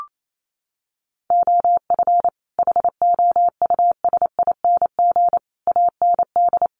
Morse code, heard as a keyed tone, about 28 wpm, 700 Hz. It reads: OF HOUSING AND
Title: Sets of 5 Words Encore2 28wpm